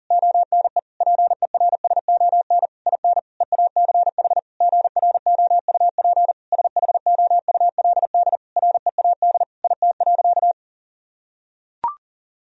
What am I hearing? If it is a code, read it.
Morse code, heard as a keyed tone, about 37 wpm, 700 Hz. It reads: ONE PERSON IN EACH GROUP SHOULD READ IT.
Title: Sentences from Top 1000 Words - Review Part 1 - 40wpm